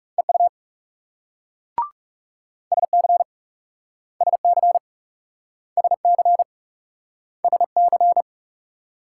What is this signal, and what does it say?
Morse code, heard as a keyed tone, about 38 wpm, 700 Hz. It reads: EU SC SC SC SC